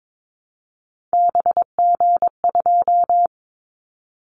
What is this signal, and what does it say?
Morse code, heard as a keyed tone, about 22 wpm, 700 Hz. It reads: BG2
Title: Callsign Prefixes for Top 100 DXCC - 22wpm